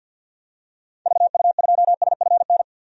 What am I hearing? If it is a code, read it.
Morse code, heard as a keyed tone, about 49 wpm, 700 Hz. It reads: VU2SFN
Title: DXCC-100 Call Signs - 6 Characters - 50wpm